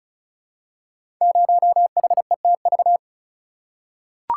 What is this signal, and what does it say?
Morse code, heard as a keyed tone, about 36 wpm, 690 Hz. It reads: 0HETV
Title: Code Groups - 5 Letter:Numbers - 35wpm